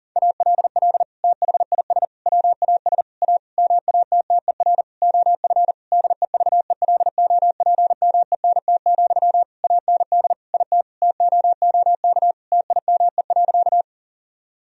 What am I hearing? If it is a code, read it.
Morse code, heard as a keyed tone, about 40 wpm, 700 Hz. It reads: ALL THIS WAS A MATTER OF DEVELOPMENT, AND IT TOOK TIME.
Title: Sentences from Top 1000 Words - Review Part 4 - 40wpm